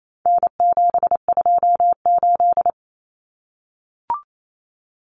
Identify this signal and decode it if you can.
Morse code, heard as a keyed tone, about 28 wpm, 700 Hz. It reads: N728
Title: Common Cut Numbers - 4-Digit 28wpm